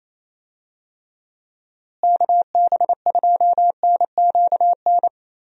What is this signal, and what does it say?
Morse code, heard as a keyed tone, about 28 wpm, 700 Hz. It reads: KB2NQN
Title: USA Call Signs - 6 Characters - Repeated 2x - 28wpm